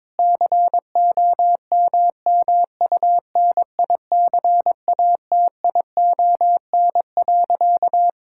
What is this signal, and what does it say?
Morse code, heard as a keyed tone, about 22 wpm, 700 Hz. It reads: COMMUNICATION.
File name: Difficult Sentences with Hints - 22wpm